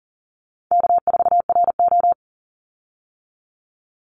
Morse code, heard as a keyed tone, about 40 wpm, 700 Hz. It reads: K4RO
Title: Call Signs - 40wpm